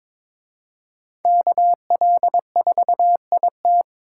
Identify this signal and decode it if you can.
Morse code, heard as a keyed tone, about 26 wpm, 700 Hz. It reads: KL4IT
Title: Speed Racing - USA 2x2 Call Signs 22wpm